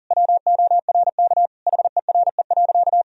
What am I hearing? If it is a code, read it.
Morse code, heard as a keyed tone, about 40 wpm, 700 Hz. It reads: WORK HERE.
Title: Sentences from Top 1000 Words - Review Part 4 - 40wpm